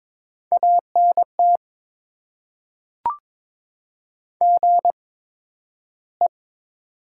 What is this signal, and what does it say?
Morse code, heard as a keyed tone, about 22 wpm, spaced as for 12 wpm, 700 Hz. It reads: ANT GE
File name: Top 20 QSO Elements - 8x Character Spacing - 22wpm